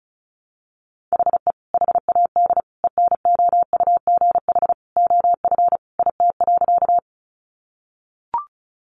Morse code, heard as a keyed tone, about 34 wpm, 700 Hz. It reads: HE HAD ENOUGH OF IT.
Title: Sentences from Top 400 Words 35wpm